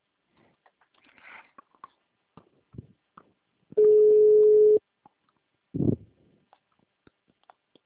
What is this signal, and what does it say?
Ringback tone (Europe, 425 Hz).